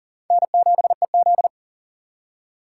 Morse code, heard as a keyed tone, about 40 wpm, 700 Hz. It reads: N7EZ
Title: USA Call Signs - 4 Characters - 3x Character Spacing - 40wpm